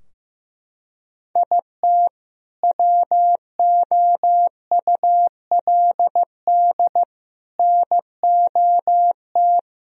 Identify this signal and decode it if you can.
Morse code, heard as a keyed tone, about 15 wpm, 700 Hz. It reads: IT WOULD NOT
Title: Sentences from Top 1000 Words - Review Part 2 - 15wpm